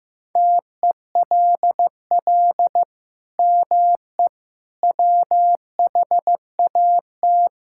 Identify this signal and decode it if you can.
Morse code, heard as a keyed tone, about 16 wpm, 700 Hz. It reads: TELL ME WHAT
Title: Sentences from Top 1600 Words 15wpm